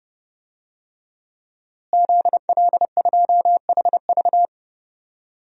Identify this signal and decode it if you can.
Morse code, heard as a keyed tone, about 30 wpm, 700 Hz. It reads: ZL2HV
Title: DXCC-100 Call Signs - 5 Characters - Repeated 3x - 30wpm